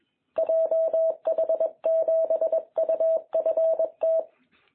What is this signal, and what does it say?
Morse code, heard as a keyed tone, about 22 wpm, 660 Hz. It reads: JH7UFT
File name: jh8cbh / 2021cw